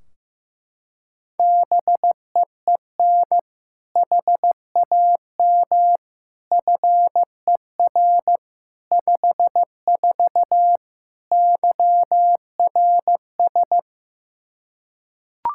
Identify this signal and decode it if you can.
Morse code, heard as a keyed tone, about 16 wpm, 700 Hz. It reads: BEEN HAM FER 54 YRS
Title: Practice QSO - Exchange 3 - More Info - 15wpm